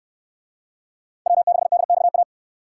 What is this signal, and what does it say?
Morse code, heard as a keyed tone, about 64 wpm, 700 Hz. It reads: W6DXA